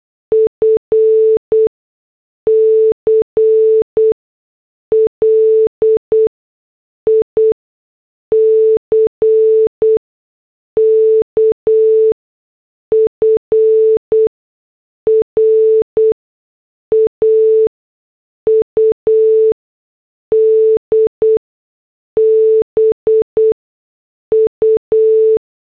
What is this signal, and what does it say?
Morse code, heard as a keyed tone, about 8 wpm, spaced as for 6 wpm, 440 Hz. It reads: F CLICK FRAUD BU
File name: morse / headlines